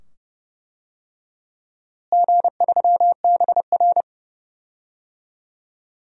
Morse code, heard as a keyed tone, about 32 wpm, 700 Hz. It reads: G3BR
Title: DXCC-100 Call Signs - 4 Characters - Repeated 2x - 30wpm